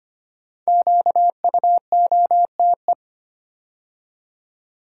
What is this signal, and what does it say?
Morse code, heard as a keyed tone, about 25 wpm, 700 Hz. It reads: QUOTE
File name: 5 Letter Words - First 3 Letters Given - 25wpm